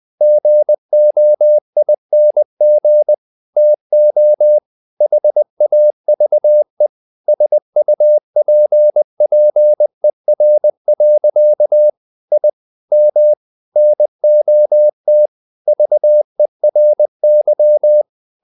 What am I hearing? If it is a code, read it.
Morse code, heard as a keyed tone, about 20 wpm, 600 Hz. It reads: GOING TO HAVE SUPPER. I M NOT VERY
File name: the old man and the sea / the old man and the sea part 1 20wpm